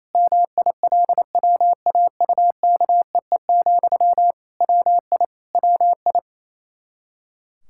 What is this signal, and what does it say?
Morse code, heard as a keyed tone, about 28 wpm, 700 Hz. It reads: MILWAUKEE, WI WI
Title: Practice QSO - Exchange 1 - Beginning - 28wpm